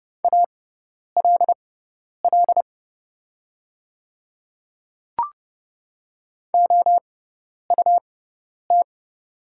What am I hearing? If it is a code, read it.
Morse code, heard as a keyed tone, about 29 wpm, spaced as for 10 wpm, 700 Hz. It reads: ALL OUT
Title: 6x Character Spacing - Top 100 Words - 30wpm